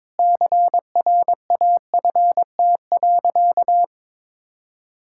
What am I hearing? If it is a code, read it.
Morse code, heard as a keyed tone, about 22 wpm, 700 Hz. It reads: CRAFT.
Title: Sentences from Top 1000 Words - Review Part 6 - 22wpm